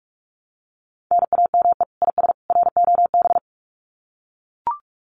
Morse code, heard as a keyed tone, about 44 wpm, 700 Hz. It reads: NAME IS ROB
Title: Common QSO Phrases 45wpm